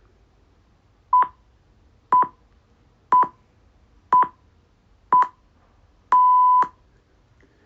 Radio time signal (time pips). It is heard from 1.1 to 6.6 s.
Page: On the 14th February 2025 the Australian Broadcasting Corporationstopped broadcasting the time pips atthe top of the hour for their analogue774 AM Radio station.